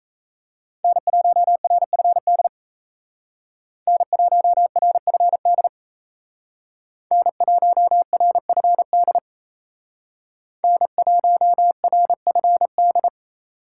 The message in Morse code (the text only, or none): N1RFD N1RFD N1RFD N1RFD